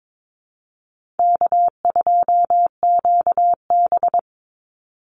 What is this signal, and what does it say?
Morse code, heard as a keyed tone, about 22 wpm, 700 Hz. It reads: K2QB
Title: Easy Call Signs Repeated 2x - 22wpm